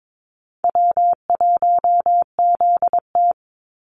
Morse code, heard as a keyed tone, about 22 wpm, 700 Hz. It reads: W1ZT
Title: Easy Call Signs - 22wpm